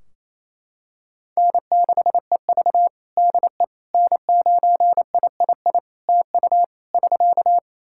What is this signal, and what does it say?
Morse code, heard as a keyed tone, about 28 wpm, 700 Hz. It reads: N6EV DE N9III TU <SK>
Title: Practice QSO - Exchange 4 - Ending - 28wpm